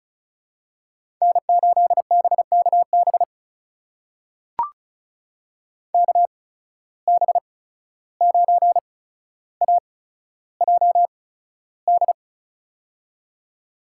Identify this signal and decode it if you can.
Morse code, heard as a keyed tone, about 36 wpm, spaced as for 14 wpm, 700 Hz. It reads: N8BKB KB9AJD
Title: USA Call Signs - 8x Character Spacing - 35wpm